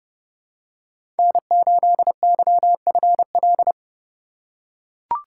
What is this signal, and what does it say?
Morse code, heard as a keyed tone, about 30 wpm, 700 Hz. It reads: N8YFL
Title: USA Call Signs - 5 Characters - 2x Character Spacing - 30wpm